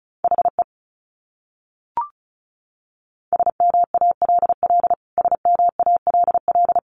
Morse code, heard as a keyed tone, about 34 wpm, 700 Hz. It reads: HE SMALL SMALL
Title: Top 300 Words Repeated 35wpm